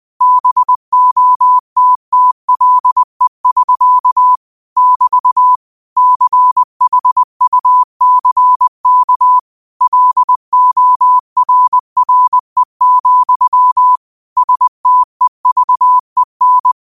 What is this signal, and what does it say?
Morse code, heard as a keyed tone, about 20 wpm, 1000 Hz. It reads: BOTTLE<SK> <BT> CHUCK LORRE, STEVEN
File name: qotdfiles / 20wpm